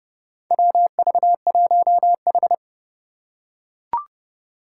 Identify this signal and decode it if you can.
Morse code, heard as a keyed tone, about 30 wpm, 700 Hz. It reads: WV1H
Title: Four Challenging Letter-Number Combinations 30wpm